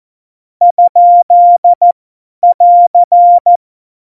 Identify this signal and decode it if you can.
Morse code, heard as a keyed tone, about 14 wpm, spaced as for 10 wpm, 700 Hz. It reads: ?<AR>
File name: Morsekurs / Witze / Witz 02. April 2019, 50 F70